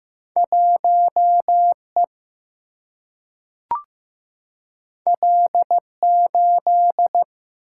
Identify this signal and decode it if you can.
Morse code, heard as a keyed tone, about 15 wpm, 700 Hz. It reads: 1E L8
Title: Single-Letter Single-Number 15wpm